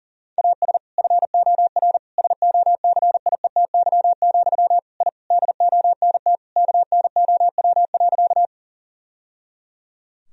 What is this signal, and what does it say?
Morse code, heard as a keyed tone, about 40 wpm, 700 Hz. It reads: AS FOR SOCIETY, I DONT KNOW.
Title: Sentences from Top 1000 Words - Review Part 4 - 40wpm